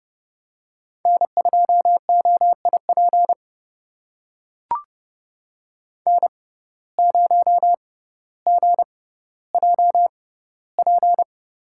Morse code, heard as a keyed tone, about 30 wpm, spaced as for 15 wpm, 700 Hz. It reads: N2OIP N0GJP
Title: USA Call Signs - 5 Characters - 6x Character Spacing - 30wpm